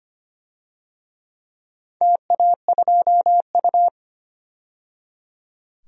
Morse code, heard as a keyed tone, about 25 wpm, 700 Hz. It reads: TA2U
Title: DXCC-100 Call Signs - 4 Characters - 25wpm